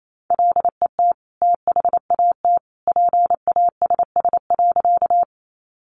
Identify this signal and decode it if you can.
Morse code, heard as a keyed tone, about 28 wpm, 700 Hz. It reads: LET THAT PASS.
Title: Sentences from Top 400 Words 28wpm